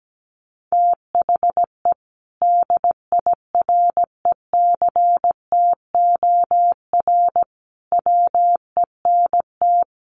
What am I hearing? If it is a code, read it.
Morse code, heard as a keyed tone, about 17 wpm, 700 Hz. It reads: THE DIRECTOR WENT
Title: Sentences from Top 1300 Words 17wpm